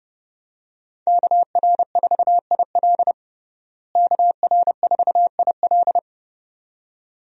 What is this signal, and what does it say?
Morse code, heard as a keyed tone, about 32 wpm, 700 Hz. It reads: KR4IL KR4IL
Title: USA Call Signs - 5 Characters - Repeated 2x - 30wpm